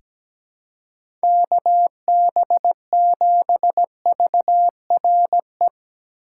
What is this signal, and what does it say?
Morse code, heard as a keyed tone, about 17 wpm, 700 Hz. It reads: KB7VRE
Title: USA Call Signs - 6 Characters - 3x Character Spacing - 17wpm